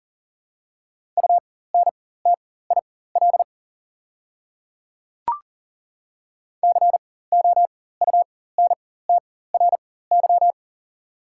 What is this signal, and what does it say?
Morse code, heard as a keyed tone, about 40 wpm, spaced as for 19 wpm, 690 Hz. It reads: UNTIL COUNTRY
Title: 4x Character Spacing - Top 300 Words - 40wpm